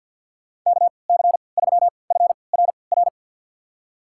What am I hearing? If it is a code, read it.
Morse code, heard as a keyed tone, about 53 wpm, spaced as for 29 wpm, 700 Hz. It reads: KC3FRR
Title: USA Call Signs - 6 Characters - 3x Character Spacing - 50wpm